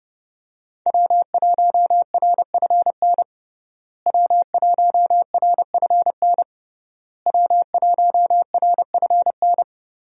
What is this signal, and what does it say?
Morse code, heard as a keyed tone, about 30 wpm, 700 Hz. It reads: W1RFN W1RFN W1RFN
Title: USA Call Signs - 5 Characters - Repeated 3x - 30wpm